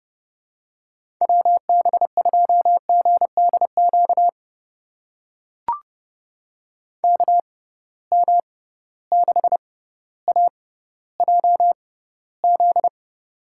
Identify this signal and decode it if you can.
Morse code, heard as a keyed tone, about 30 wpm, spaced as for 15 wpm, 700 Hz. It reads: WB2GDQ KM6AJZ
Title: USA Call Signs - 6 Characters - 6x Character Spacing - 30wpm